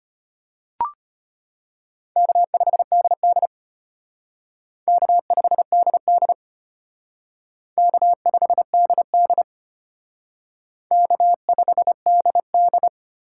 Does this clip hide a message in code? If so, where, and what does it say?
Yes: Morse code from 2.2 to 12.9 s, heard as a keyed tone, about 32 wpm, 700 Hz. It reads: K5DD K5DD K5DD K5DD